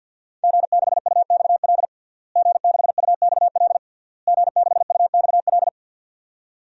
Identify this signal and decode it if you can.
Morse code, heard as a keyed tone, about 49 wpm, 700 Hz. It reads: G6UXL G6UXL G6UXL
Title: DXCC-100 Call Signs - 5 Characters - Repeated 3x - 50wpm